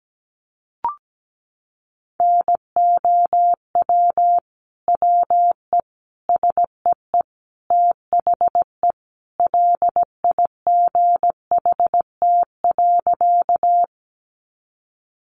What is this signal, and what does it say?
Morse code, heard as a keyed tone, about 17 wpm, 700 Hz. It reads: NOW WE SEE THE LIGHT.
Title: Sentences from Top 200 Words 17wpm